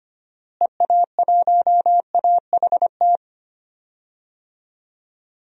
Morse code, heard as a keyed tone, about 25 wpm, 700 Hz. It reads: EA1AHT
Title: DXCC-100 Call Signs - 6 Characters - Repeated 2x - 25wpm